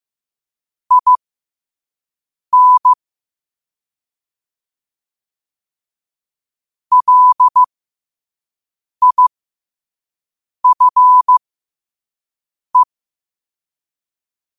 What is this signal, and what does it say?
Morse code, heard as a keyed tone, about 15 wpm, spaced as for 5 wpm, 1000 Hz. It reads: IN LIFE
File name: qotdfiles / 5wpm